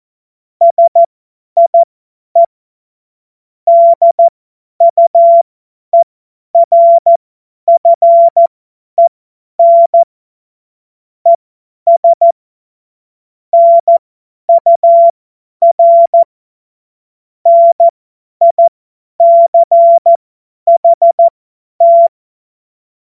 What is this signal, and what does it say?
Morse code, heard as a keyed tone, about 14 wpm, spaced as for 10 wpm, 680 Hz. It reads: SIE DUERFEN ES NUR NICHT
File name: Morsekurs / Witze / Witz 19 März 2024 50 F70